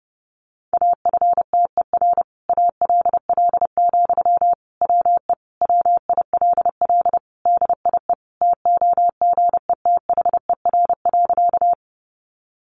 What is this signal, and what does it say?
Morse code, heard as a keyed tone, about 30 wpm, 700 Hz. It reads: AFTER ALL, WE WILL DIE TOGETHER.
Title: Sentences from Top 1300 Words 30wpm